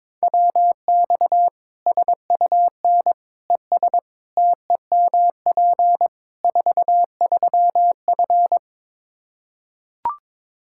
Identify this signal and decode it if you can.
Morse code, heard as a keyed tone, about 22 wpm, 700 Hz. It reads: WX SUN ES TEMP 43F
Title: Practice QSO - Exchange 2 - Info - 22wpm